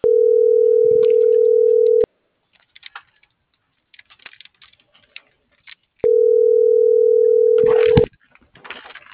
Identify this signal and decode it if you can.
Ringback tone (North America), 440 + 480 Hz.